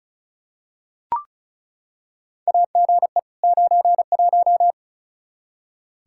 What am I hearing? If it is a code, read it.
Morse code, heard as a keyed tone, about 34 wpm, 700 Hz. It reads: AGE 91
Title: Practice QSO - Exchange 3 - More Info - 35wpm